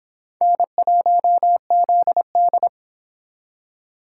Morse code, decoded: N1ZD

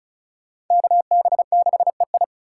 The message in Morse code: KB6EI

